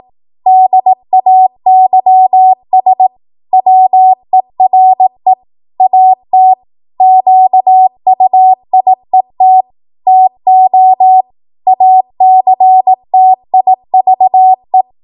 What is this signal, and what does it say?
Morse code, heard as a keyed tone, about 18 wpm, 750 Hz. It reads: DAYS WERE AT QUIET TO ACTIVE
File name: Morse / 18WPM